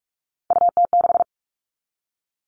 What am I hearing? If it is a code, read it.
Morse code, heard as a keyed tone, about 44 wpm, 700 Hz. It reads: UT6
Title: All Cut Numbers - 3-Digit 45wpm